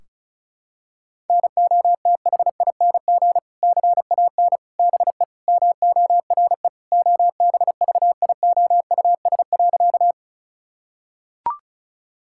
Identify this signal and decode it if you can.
Morse code, heard as a keyed tone, about 34 wpm, 700 Hz. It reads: NOTHING CAN BE MORE OBVIOUS.
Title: Sentences from Top 1600 Words 35wpm